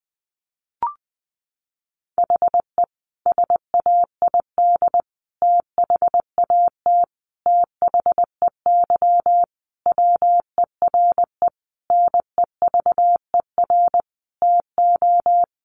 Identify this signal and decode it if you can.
Morse code, heard as a keyed tone, about 20 wpm, 700 Hz. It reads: HE SAID THAT THEY WERE NEVER TO
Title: Sentences from Top 800 Words 20wpm